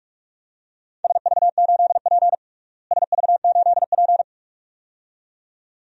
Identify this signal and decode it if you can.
Morse code, heard as a keyed tone, about 46 wpm, 700 Hz. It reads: SV8P SV8P
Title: DXCC-100 Call Signs - 4 Characters - Repeated 2x - 45wpm